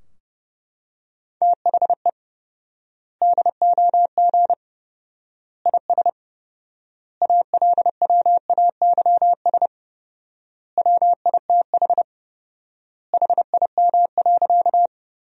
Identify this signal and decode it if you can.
Morse code, heard as a keyed tone, about 30 wpm, 700 Hz. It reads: THE DOG IS ALWAYS WITH HIM.
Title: 4x Word-Spacing - Sentences from Top 400 Words - 30wpm